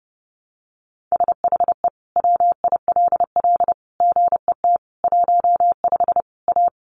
Morse code, heard as a keyed tone, about 30 wpm, 700 Hz. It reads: SHE WILL GET 15 A
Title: Easy Phrases 30wpm